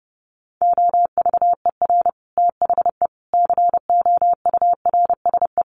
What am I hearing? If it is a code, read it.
Morse code, heard as a keyed tone, about 30 wpm, 700 Hz. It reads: OVER THE COURSE
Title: Sets of 4 Words 30wpm